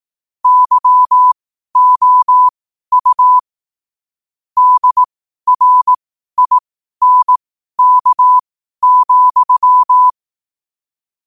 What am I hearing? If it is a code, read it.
Morse code, heard as a keyed tone, about 18 wpm, spaced as for 12 wpm, 1000 Hz. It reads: YOU DRINK,
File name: qotdfiles / 12wpm